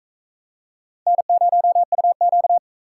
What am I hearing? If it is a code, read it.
Morse code, heard as a keyed tone, about 36 wpm, 700 Hz. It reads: N0UQ